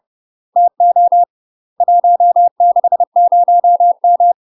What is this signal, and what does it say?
Morse code, heard as a keyed tone, about 30 wpm, 700 Hz. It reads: TO 160M
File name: UK MASTER QSOs TRAINING / 30WPM / UK MASTER QSOs 0018